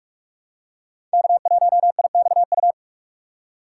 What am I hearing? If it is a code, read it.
Morse code, heard as a keyed tone, about 46 wpm, 700 Hz. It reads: K1IXU
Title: USA Call Signs - 5 Characters - 2x Character Spacing - 45wpm